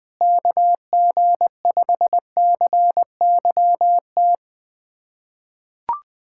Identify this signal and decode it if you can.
Morse code, heard as a keyed tone, about 21 wpm, 700 Hz. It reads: KG5CYT
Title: USA Callsign DE Callsign - 6 Characters - 20wpm